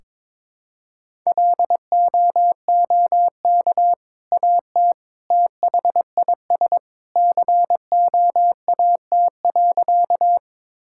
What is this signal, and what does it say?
Morse code, heard as a keyed tone, about 22 wpm, 700 Hz. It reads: LOOK AT THIS COAT.
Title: Sentences from Top 1600 Words 22wpm